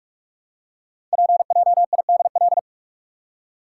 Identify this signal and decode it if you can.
Morse code, heard as a keyed tone, about 44 wpm, 700 Hz. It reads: PJIDL
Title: Code Groups - 5 Letters - 45wpm